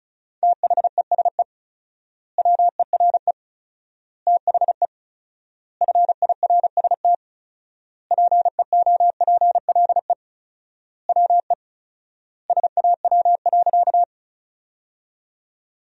Morse code, heard as a keyed tone, about 36 wpm, 700 Hz. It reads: THESE WERE THE FIRST PEOPLE WE SAW.
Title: 4x Word-Spacing - Sentences from Top 300 Words - 35wpm